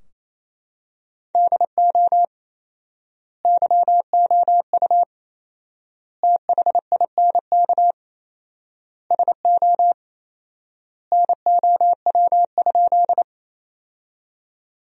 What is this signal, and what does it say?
Morse code, heard as a keyed tone, about 28 wpm, 700 Hz. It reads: DO YOU THINK SO NOW?
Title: 4x Word-Spacing - Sentences from Top 200 Words - 28wpm